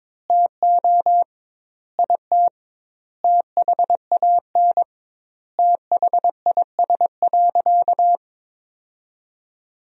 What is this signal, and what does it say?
Morse code, heard as a keyed tone, about 21 wpm, 700 Hz. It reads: TO IT THAN THIS.
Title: 2x Word-Spacing - Sentences from Top 100 Words - 22wpm